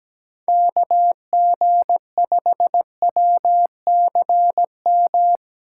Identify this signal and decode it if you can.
Morse code, heard as a keyed tone, about 17 wpm, 700 Hz. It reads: KG5WCM
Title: USA Call Signs - 6 Characters - 8x Character Spacing - 17wpm